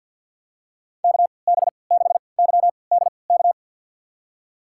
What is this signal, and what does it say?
Morse code, heard as a keyed tone, about 52 wpm, spaced as for 29 wpm, 700 Hz. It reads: KB6YDK
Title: USA Call Signs - 6 Characters - 3x Character Spacing - 50wpm